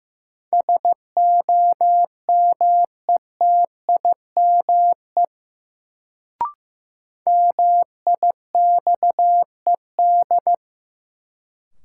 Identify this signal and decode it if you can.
Morse code, heard as a keyed tone, about 15 wpm, 700 Hz. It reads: SOMETIME MIXED
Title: Sentences from Top 1500 Words - New Word Drill Alt - 15wpm